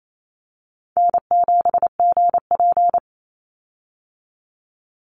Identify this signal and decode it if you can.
Morse code, heard as a keyed tone, about 28 wpm, 700 Hz. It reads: N7GP
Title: USA Call Signs 28wpm